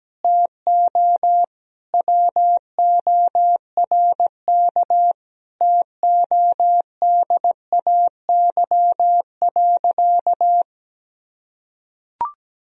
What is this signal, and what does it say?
Morse code, heard as a keyed tone, about 17 wpm, 700 Hz. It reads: TO WORK TODAY.
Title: Sentences from Top 1000 Words - Review Part 4 - 17wpm